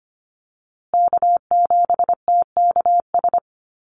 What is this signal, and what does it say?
Morse code, heard as a keyed tone, about 25 wpm, 700 Hz. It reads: K7TKS
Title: USA Call Signs Repeated 3x 25wpm